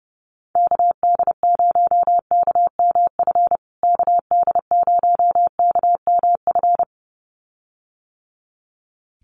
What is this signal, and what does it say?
Morse code, heard as a keyed tone, about 30 wpm, 700 Hz. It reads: KD0KMF KD0KMF
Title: USA Call Signs Repeated 2x 30wpm